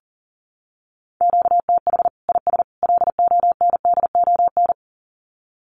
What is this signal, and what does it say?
Morse code, heard as a keyed tone, about 40 wpm, 700 Hz. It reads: QTH IS LONDON
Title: Common QSO Phrases 40wpm